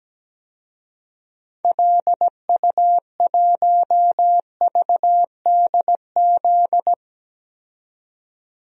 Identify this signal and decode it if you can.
Morse code, heard as a keyed tone, about 17 wpm, 700 Hz. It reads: LU1VDZ
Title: DXCC-100 Call Signs - 6 Characters - 17wpm